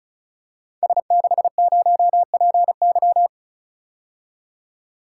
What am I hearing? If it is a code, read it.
Morse code, heard as a keyed tone, about 36 wpm, 700 Hz. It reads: S60PY
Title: Code Groups - 5 Letter:Numbers - 35wpm